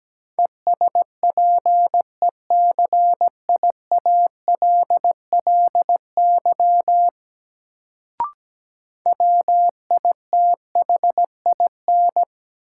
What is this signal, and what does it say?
Morse code, heard as a keyed tone, about 17 wpm, 700 Hz. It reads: ESPECIALLY WITHIN
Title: Sentences from Top 700 Words - New Word Drill - 17wpm